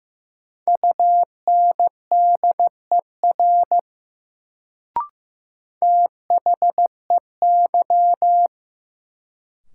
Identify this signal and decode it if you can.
Morse code, heard as a keyed tone, about 15 wpm, 700 Hz. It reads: UNDER THEY
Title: Sets of 2 Words - New Word Drill - 15wpm